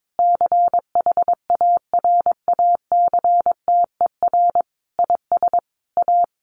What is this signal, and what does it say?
Morse code, heard as a keyed tone, about 22 wpm, 700 Hz. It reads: CHARACTER IS A
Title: Additional Easy Phrases 22wpm